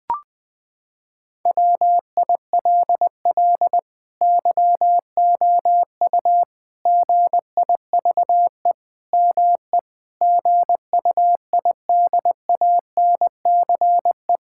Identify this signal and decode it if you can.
Morse code, heard as a keyed tone, about 21 wpm, 700 Hz. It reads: WILL YOU GIVE ME GUIDANCE
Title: Sentences from Top 1600 Words 20wpm